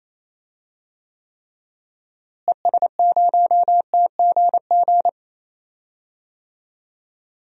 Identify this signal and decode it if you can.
Morse code, heard as a keyed tone, about 28 wpm, 700 Hz. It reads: ES0TGG
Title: DXCC-100 Call Signs - 6 Characters - Repeated 3x - 28wpm